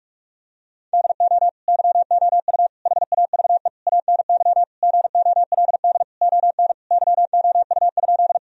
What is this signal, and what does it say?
Morse code, heard as a keyed tone, about 48 wpm, 700 Hz. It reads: DO YOU HAVE ANY GOLD ON YOU?
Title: Sentences from Top 1000 Words - Review Part 3 - 45wpm